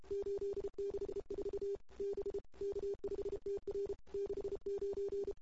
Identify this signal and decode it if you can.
Morse code, heard as a keyed tone, about 33 wpm, 440 Hz. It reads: 864 B K5TR 69
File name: CW / 2008-ss-cw-k5tr-op-n5rz -06:00